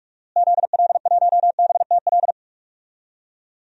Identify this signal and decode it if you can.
Morse code, heard as a keyed tone, about 42 wpm, 700 Hz. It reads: ZL1BTL
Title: DXCC-100 Call Signs - 6 Characters - Repeated 3x - 45wpm